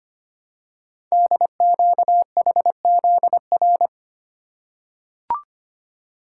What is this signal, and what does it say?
Morse code, heard as a keyed tone, about 25 wpm, 700 Hz. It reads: DQHZR
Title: Code Groups - 5 Letter:Numbers - 25wpm